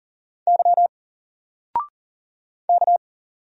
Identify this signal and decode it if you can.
Morse code, heard as a keyed tone, about 40 wpm, 700 Hz. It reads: Y K
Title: Single Letter-Number 40wpm